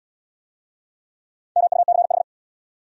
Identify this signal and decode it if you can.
Morse code, heard as a keyed tone, about 67 wpm, 700 Hz. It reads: MUCH